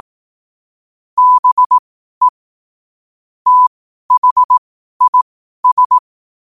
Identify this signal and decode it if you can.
Morse code, heard as a keyed tone, about 18 wpm, spaced as for 12 wpm, 1000 Hz. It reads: BE THIS
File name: qotdfiles / 12wpm